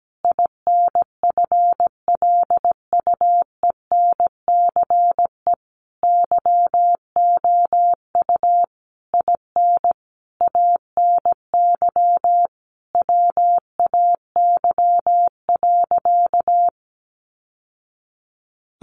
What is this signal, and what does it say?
Morse code, heard as a keyed tone, about 17 wpm, 700 Hz. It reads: INFLUENCE YOU IN ANY WAY.
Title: Sentences from Top 1100 Words 17wpm